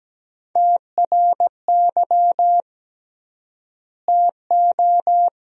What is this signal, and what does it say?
Morse code, heard as a keyed tone, about 17 wpm, 700 Hz. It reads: TRY TO
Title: 3x Word-Spacing - Sentences from Top 500 Words - 17wpm